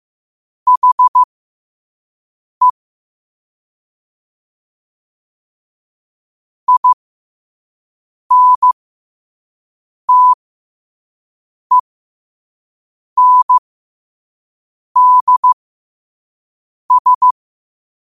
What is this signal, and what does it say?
Morse code, heard as a keyed tone, about 15 wpm, spaced as for 5 wpm, 1000 Hz. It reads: HE INTENDS
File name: qotdfiles / 5wpm